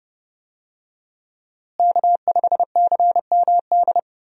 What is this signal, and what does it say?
Morse code, heard as a keyed tone, about 30 wpm, 700 Hz. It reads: K5CMD
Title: USA Call Signs - 5 Characters - Repeated 2x - 30wpm